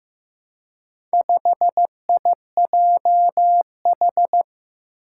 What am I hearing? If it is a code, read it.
Morse code, heard as a keyed tone, about 15 wpm, 700 Hz. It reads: 5IJH
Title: Four Challenging Letter-Number Combinations 15wpm